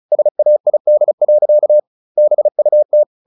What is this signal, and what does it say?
Morse code, heard as a keyed tone, about 34 wpm, 600 Hz. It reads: SAID. BUT
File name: the old man and the sea 2 35wpm